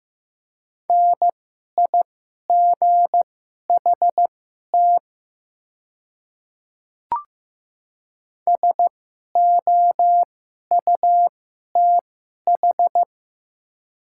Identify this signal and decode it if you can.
Morse code, heard as a keyed tone, about 15 wpm, spaced as for 11 wpm, 700 Hz. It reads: NIGHT SOUTH
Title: 2x Character Spacing - Top 500 Words - 15wpm